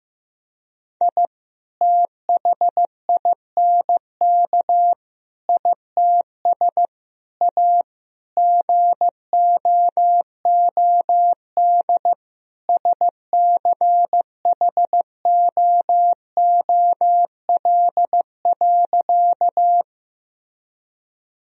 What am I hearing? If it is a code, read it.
Morse code, heard as a keyed tone, about 15 wpm, 700 Hz. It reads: I THINK ITS A GOOD SCHOOL.
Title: Sentences from Top 1000 Words - Review Part 1 - 15wpm